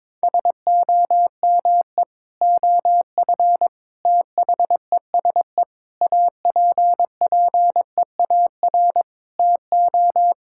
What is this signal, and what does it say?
Morse code, heard as a keyed tone, about 21 wpm, 700 Hz. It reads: SOME OF THESE APPEAR TO
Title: Sentences from Top 1000 Words - Review Part 6 - 22wpm